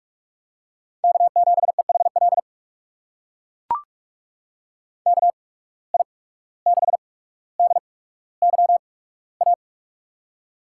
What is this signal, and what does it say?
Morse code, heard as a keyed tone, about 44 wpm, spaced as for 18 wpm, 690 Hz. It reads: K7EHL KI6DYA
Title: USA Call Signs - 8x Character Spacing - 45wpm